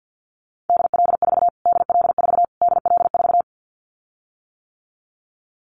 Morse code, heard as a keyed tone, about 50 wpm, 700 Hz. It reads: DL4 DL4 DL4
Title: Callsign Prefixes for Top 25 DXCC - Repeated 3x - 50wpm